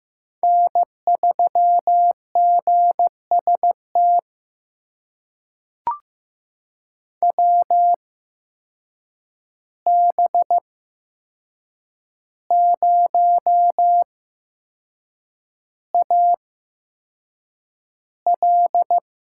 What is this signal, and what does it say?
Morse code, heard as a keyed tone, about 15 wpm, spaced as for 6 wpm, 700 Hz. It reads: N3GST WB0AL
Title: USA Call Signs - 5 Characters - 8x Character Spacing - 15wpm